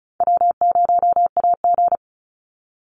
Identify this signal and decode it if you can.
Morse code, heard as a keyed tone, about 34 wpm, 700 Hz. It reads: W0AG
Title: Call Signs - 35wpm